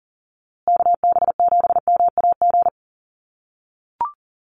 Morse code, heard as a keyed tone, about 40 wpm, 700 Hz. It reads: KB7MAG
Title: USA Call Signs Repeated 3x 40wpm